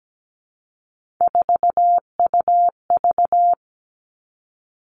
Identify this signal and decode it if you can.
Morse code, heard as a keyed tone, about 17 wpm, 700 Hz. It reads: 4UV
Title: Three Challenging Letter-Number Combinations 17wpm